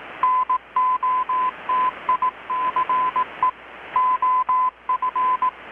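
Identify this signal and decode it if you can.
Morse code, heard as a keyed tone, about 18 wpm, 1010 Hz, with radio noise. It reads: NOTICE OF
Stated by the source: Morse from KPH's Night of Nights XX maritime radiotelegraphy event, received in Chiapas, Mexico about 3,800 km away on a Kenwood R-600 with a grounded 10 m long-wire antenna.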